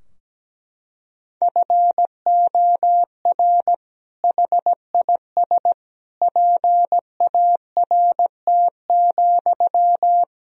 Morse code, heard as a keyed tone, about 17 wpm, 700 Hz. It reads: FOR HIS PART,
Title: Sentences from Top 1000 Words - Review Part 2 - 17wpm